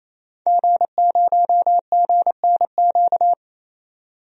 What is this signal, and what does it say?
Morse code, heard as a keyed tone, about 28 wpm, 700 Hz. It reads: G0GNQ
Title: DXCC-100 Call Signs - 5 Characters - 28wpm